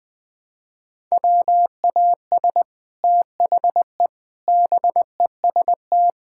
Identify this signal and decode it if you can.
Morse code, heard as a keyed tone, about 20 wpm, 700 Hz. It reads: WAS THE BEST
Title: Sets of 8 Words - 20wpm